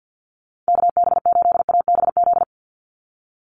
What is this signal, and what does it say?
Morse code, heard as a keyed tone, about 50 wpm, 700 Hz. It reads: KB8ABZ
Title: USA Call Signs 50wpm